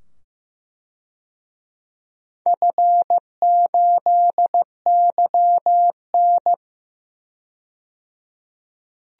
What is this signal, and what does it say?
Morse code, heard as a keyed tone, about 15 wpm, 700 Hz. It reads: F8YN
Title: DXCC-100 Call Signs - 4 Characters - 15wpm